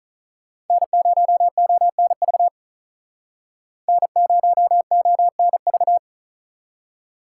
Morse code, heard as a keyed tone, about 46 wpm, 700 Hz. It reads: N0ONV N0ONV